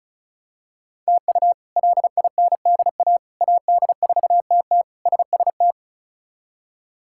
Morse code, heard as a keyed tone, about 36 wpm, 700 Hz. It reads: TU LINDA AD4TT SST
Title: K1USN SST QSO Practice - 35wpm